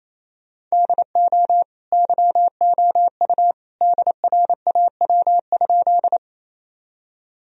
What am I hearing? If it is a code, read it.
Morse code, heard as a keyed tone, about 28 wpm, 700 Hz. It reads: DO YOU DRAW?
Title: Sentences from Top 1000 Words - Review Part 1 - 28wpm